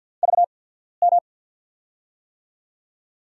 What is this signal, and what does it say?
Morse code, heard as a keyed tone, about 49 wpm, spaced as for 14 wpm, 700 Hz. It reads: VM
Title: USA Call Signs - 6 Characters - 8x Character Spacing - 50wpm